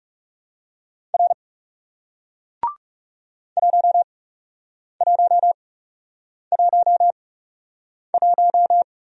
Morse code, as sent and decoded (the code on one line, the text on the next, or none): .-. / .---- .---- .---- .----
R 1111